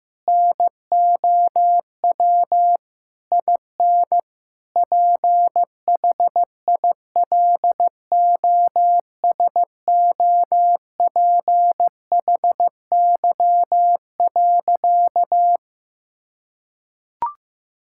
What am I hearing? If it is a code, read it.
Morse code, heard as a keyed tone, about 16 wpm, 700 Hz. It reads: NOW IN PHILOSOPHY.
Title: Sentences from Top 1000 Words - Review Part 6 - 15wpm